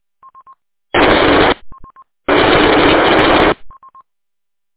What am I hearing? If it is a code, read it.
There is no code.